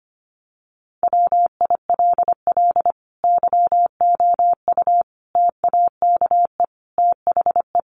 Morse code, heard as a keyed tone, about 25 wpm, 700 Hz. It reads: WILL YOU TAKE THE
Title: Sentences from Top 1300 Words 25wpm